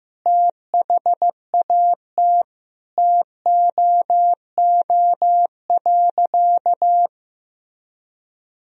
Morse code, heard as a keyed tone, about 15 wpm, 700 Hz. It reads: THAT TOO.
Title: Sentences from Top 1000 Words - Review Part 1 - 15wpm